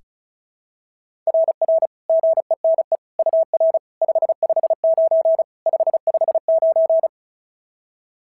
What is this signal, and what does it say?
Morse code, heard as a keyed tone, about 36 wpm, 690 Hz. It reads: RR GENE UR 559 559
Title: SOTA QSO Practice 35wpm